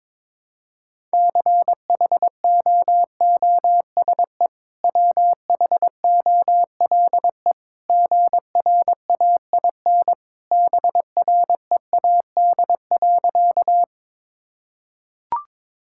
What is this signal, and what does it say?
Morse code, heard as a keyed tone, about 22 wpm, 700 Hz. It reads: CHOOSE WHOLE GRAIN BREAD.
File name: practice / 22wpm / Easy Sentences with Hints - 22wpm